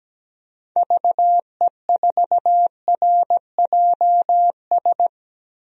Morse code, heard as a keyed tone, about 17 wpm, 700 Hz. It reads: VE4RJS
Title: DXCC-100 Call Signs - 6 Characters - Repeated 3x - 17wpm